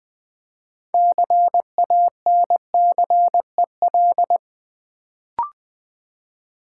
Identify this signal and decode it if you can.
Morse code, heard as a keyed tone, about 20 wpm, 700 Hz. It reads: CANCEL
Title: Build A Word - 2 Letters At A Time - 6-Letter Words - 20wpm